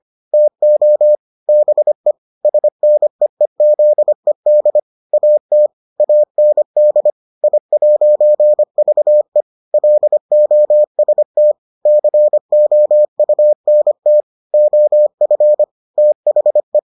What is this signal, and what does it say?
Morse code, heard as a keyed tone, about 25 wpm, 600 Hz. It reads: TO BE SNEEZED AT AND I'VE LOST COUNT OF THE